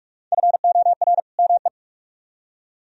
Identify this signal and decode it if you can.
Morse code, heard as a keyed tone, about 49 wpm, 700 Hz. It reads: FOR ME